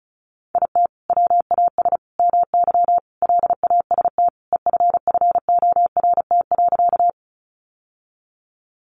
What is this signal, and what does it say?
Morse code, heard as a keyed tone, about 34 wpm, 700 Hz. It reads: IT WAS MY LAST EFFORT.
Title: Sentences from Top 1200 Words 35wpm